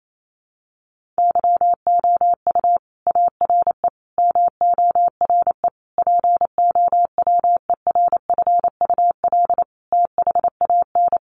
Morse code, heard as a keyed tone, about 28 wpm, 700 Hz. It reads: YOU ARE MORE POWERFUL THAN
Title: Sentences from Top 1300 Words 28wpm